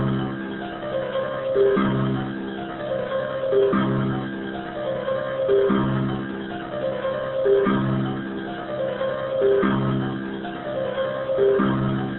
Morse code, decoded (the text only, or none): none